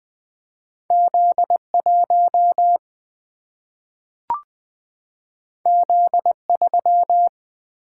Morse code, heard as a keyed tone, about 20 wpm, 700 Hz. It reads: Z 1 Z 3
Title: Single-Letter Single-Number 20wpm